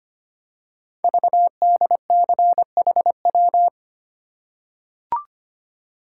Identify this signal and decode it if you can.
Morse code, heard as a keyed tone, about 25 wpm, 700 Hz. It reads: VDCHW
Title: Code Groups - 5 Letters - 25wpm